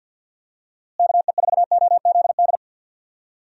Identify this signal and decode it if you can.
Morse code, heard as a keyed tone, about 49 wpm, 700 Hz. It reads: KE4OZD
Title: USA Call Signs - 6 Characters - Repeated 3x - 50wpm